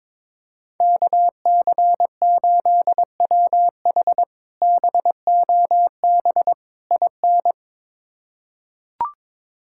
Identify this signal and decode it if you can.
Morse code, heard as a keyed tone, about 22 wpm, 700 Hz. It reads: KC8WH BOB IN
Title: K1USN SST QSO Practice - 22wpm